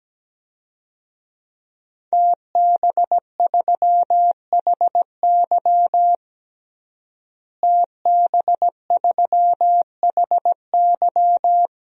Morse code, heard as a keyed tone, about 17 wpm, 700 Hz. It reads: TB3HY TB3HY
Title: DXCC-100 Call Signs - 5 Characters - Repeated 3x - 17wpm